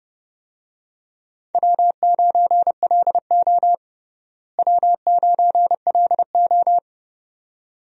Morse code, heard as a keyed tone, about 30 wpm, 700 Hz. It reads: W9LO W9LO
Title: USA Call Signs - 4 Characters - Repeated 2x - 30wpm